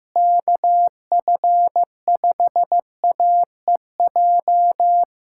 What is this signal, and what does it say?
Morse code, heard as a keyed tone, about 15 wpm, 700 Hz. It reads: KF5AEJ
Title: USA Callsign DE Callsign - 6 Characters - 15wpm